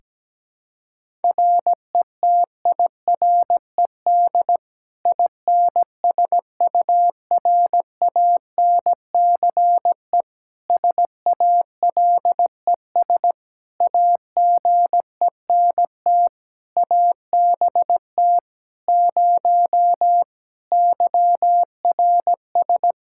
Morse code, heard as a keyed tone, about 17 wpm, 700 Hz. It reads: RETIRED INSURANCE SALES AGENT ABT 0 YRS
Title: Practice QSO - Exchange 3 - More Info - 17wpm